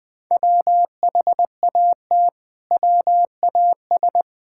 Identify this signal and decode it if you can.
Morse code, heard as a keyed tone, about 20 wpm, 700 Hz. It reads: WHAT WAS
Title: Build A Phrase - 4-Word Phrases - 20wpm